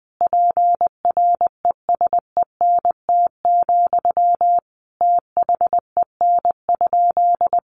Morse code, heard as a keyed tone, about 20 wpm, 700 Hz. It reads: PRESENT, THEN?
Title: Sentences from Top 800 Words 20wpm